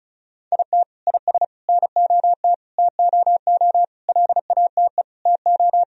Morse code, heard as a keyed tone, about 34 wpm, 700 Hz. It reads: IT IS NOT TOO LATE TO
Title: Sentences from Top 1000 Words - Review Part 2 - 35wpm